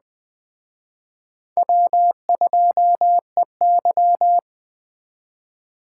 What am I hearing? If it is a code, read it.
Morse code, heard as a keyed tone, about 20 wpm, 700 Hz. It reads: W2EY